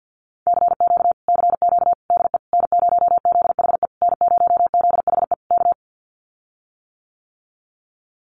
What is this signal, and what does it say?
Morse code, heard as a keyed tone, about 48 wpm, 700 Hz. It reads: CQ CQ DE N0ZHE N0ZHE K
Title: Common QSO Phrases Encore 50wpm